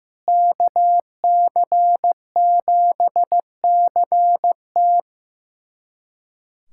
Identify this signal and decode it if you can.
Morse code, heard as a keyed tone, about 15 wpm, 700 Hz. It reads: KC7CT
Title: USA Callsign DE Callsign - 5 Characters - 15wpm